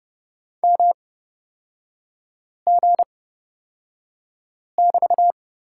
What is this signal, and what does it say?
Morse code, heard as a keyed tone, about 31 wpm, spaced as for 5 wpm, 700 Hz. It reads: MG<BT>
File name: Lesson 36e - BT - Three Characters 30wpm